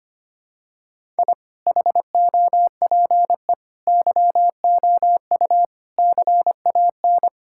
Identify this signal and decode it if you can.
Morse code, heard as a keyed tone, about 25 wpm, 700 Hz. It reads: I HOPE YOU CAN
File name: practice / 25wpm / Sentences with 9-Word Phrases - 25wpm